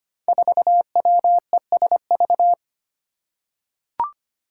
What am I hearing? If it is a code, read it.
Morse code, heard as a keyed tone, about 25 wpm, 700 Hz. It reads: 4WESV
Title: Five Challenging Letter-Number Combinations 25wpm